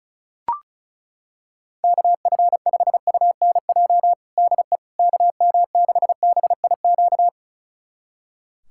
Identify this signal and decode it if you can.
Morse code, heard as a keyed tone, about 36 wpm, 700 Hz. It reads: KF5UNJ DE KM6BIQ
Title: USA Callsign DE Callsign 35wpm